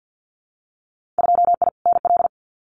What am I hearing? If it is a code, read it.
Morse code, heard as a keyed tone, about 50 wpm, 700 Hz. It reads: 2I NL
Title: Winter Field Day Exchanges Encore 50wpm